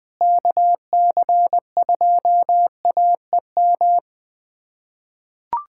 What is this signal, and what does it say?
Morse code, heard as a keyed tone, about 21 wpm, 700 Hz. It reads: KC2AEM
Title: USA Callsign DE Callsign 20wpm